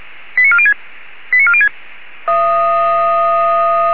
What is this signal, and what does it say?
Five-tone selective call (ZVEI), heard as quick runs of five tones, tones of about 70 ms. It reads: 89406 89406